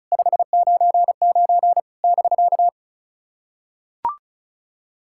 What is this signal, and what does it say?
Morse code, heard as a keyed tone, about 36 wpm, 700 Hz. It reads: 599 <BK>
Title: SOTA QSO Practice 35wpm